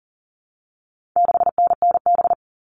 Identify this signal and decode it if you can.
Morse code, heard as a keyed tone, about 40 wpm, 700 Hz. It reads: 6NNB
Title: All Cut Numbers - 4-Digit 40wpm